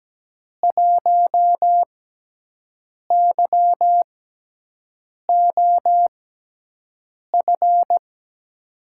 Morse code, heard as a keyed tone, about 17 wpm, spaced as for 6 wpm, 700 Hz. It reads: 1YOF
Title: USA Call Signs - 6x Character Spacing - 17wpm